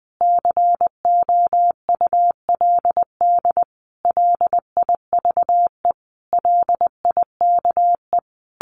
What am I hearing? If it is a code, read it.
Morse code, heard as a keyed tone, about 20 wpm, 700 Hz. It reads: COULD LIVE LIKE
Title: Sentences from Top 1400 Words 20wpm